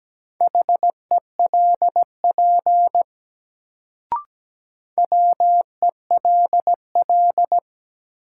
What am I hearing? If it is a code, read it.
Morse code, heard as a keyed tone, about 17 wpm, 700 Hz. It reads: HELP WELL
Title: Sentences from Top 200 Words - New Word Drill - 17wpm